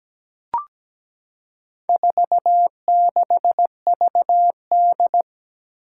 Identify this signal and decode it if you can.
Morse code, heard as a keyed tone, about 17 wpm, 700 Hz. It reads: 46VD
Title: Four Challenging Letter-Number Combinations 17wpm